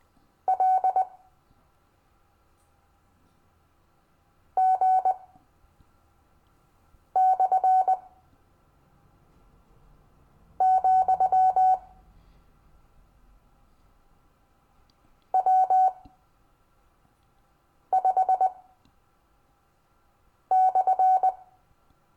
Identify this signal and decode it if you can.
Morse code, heard as a keyed tone, about 20 wpm, spaced as for 4 wpm, 750 Hz. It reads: L G/, W5/
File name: cw / Lesson15 five forwardslash